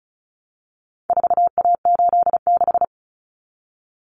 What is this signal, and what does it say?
Morse code, heard as a keyed tone, about 34 wpm, 700 Hz. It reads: 4A86
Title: Common Cut Numbers - 4-Digit 35wpm